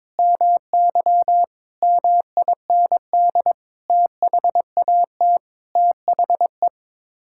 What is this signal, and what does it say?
Morse code, heard as a keyed tone, about 22 wpm, 700 Hz. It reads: MY MIND THAT THE
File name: Sets of 9 Words - 22wpm